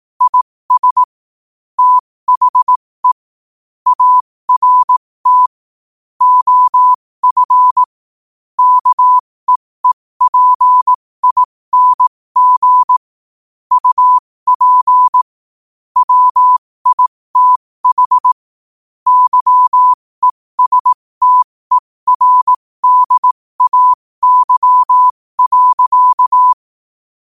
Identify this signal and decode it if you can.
Morse code, heard as a keyed tone, about 18 wpm, spaced as for 15 wpm, 1000 Hz. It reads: IS THE ART OF KEEPING UP WITH YESTERDAY.
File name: qotdfiles / 15wpm